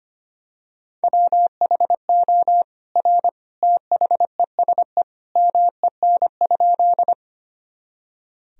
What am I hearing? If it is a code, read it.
Morse code, heard as a keyed tone, about 25 wpm, 700 Hz. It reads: WHO R THESE MEN?
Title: Sentences from Top 200 Words and Abbreviations 25wpm